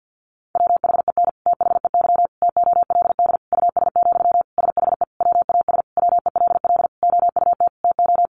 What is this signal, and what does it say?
Morse code, heard as a keyed tone, about 50 wpm, 700 Hz. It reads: WHEN THEY TOLD US, SHE WAS WELL OUT TO
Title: Sentences from Top 400 Words 50wpm